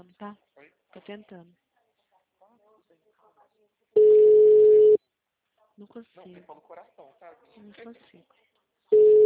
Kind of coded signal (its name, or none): telephone call progress tone